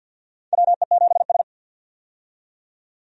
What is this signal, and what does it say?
Morse code, heard as a keyed tone, about 47 wpm, 700 Hz. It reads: WE7S
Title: USA Call Signs - 4 Characters - 50wpm